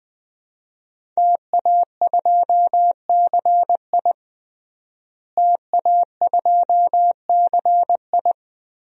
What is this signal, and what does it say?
Morse code, heard as a keyed tone, about 20 wpm, 700 Hz. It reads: TA2CI TA2CI
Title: DXCC-100 Call Signs - 5 Characters - Repeated 2x - 20wpm